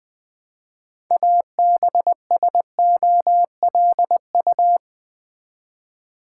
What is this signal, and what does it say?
Morse code, heard as a keyed tone, about 20 wpm, 700 Hz. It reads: ABSOLU
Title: Build A Word - 2 Letters At A Time - 8-Letter Words - 20wpm